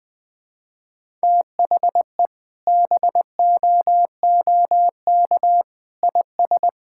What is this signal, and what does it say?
Morse code, heard as a keyed tone, about 20 wpm, 700 Hz. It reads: THE BOOK IS
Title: Sentences with 2-Word Phrases - Spoken Beginning - 20wpm